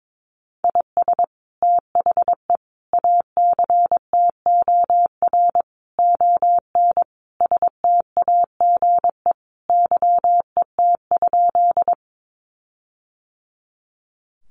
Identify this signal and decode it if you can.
Morse code, heard as a keyed tone, about 22 wpm, 700 Hz. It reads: IS THE ACTOR ON STAGE YET?
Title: Sentences from Top 1500 Words 22wpm